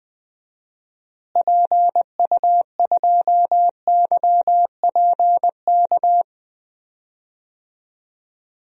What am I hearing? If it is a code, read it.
Morse code, heard as a keyed tone, about 20 wpm, 700 Hz. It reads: PU2YPK
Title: DXCC-100 Call Signs - 6 Characters - Repeated 2x - 20wpm